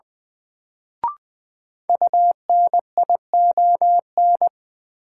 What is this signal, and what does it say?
Morse code, heard as a keyed tone, about 20 wpm, 700 Hz. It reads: UNION
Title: Sentences from Top 1400 Words - New Word Drill - 20wpm